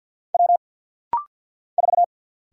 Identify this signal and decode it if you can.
Morse code, heard as a keyed tone, about 49 wpm, 700 Hz. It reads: W 4
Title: Single Letter-Number - Rapid-Fire 50wpm